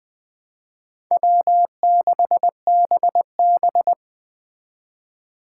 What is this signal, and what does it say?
Morse code, heard as a keyed tone, about 21 wpm, 700 Hz. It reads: W6BB